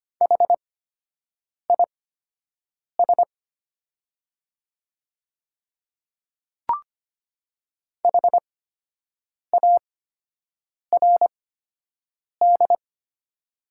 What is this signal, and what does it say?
Morse code, heard as a keyed tone, about 25 wpm, spaced as for 7 wpm, 700 Hz. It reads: HIS HARD
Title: 8x Character Spacing - Top 500 Words - 25wpm